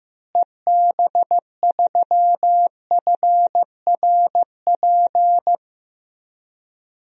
Morse code, heard as a keyed tone, about 15 wpm, 700 Hz. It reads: EB3FRP
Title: DXCC-100 Call Signs - 6 Characters - Repeated 2x - 15wpm